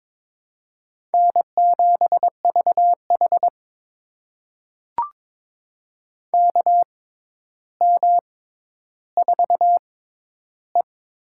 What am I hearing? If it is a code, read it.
Morse code, heard as a keyed tone, about 22 wpm, spaced as for 11 wpm, 700 Hz. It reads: N7VH KM4E
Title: USA Call Signs - 4 Characters - 6x Character Spacing - 22wpm